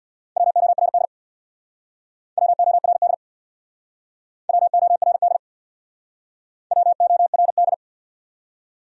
Morse code, heard as a keyed tone, about 65 wpm, 700 Hz. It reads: WORD WORD WORD WORD